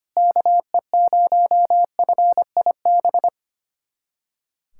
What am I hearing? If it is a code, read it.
Morse code, heard as a keyed tone, about 25 wpm, 700 Hz. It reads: KE0FIB
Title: USA Callsign DE Callsign - 6 Characters - 25wpm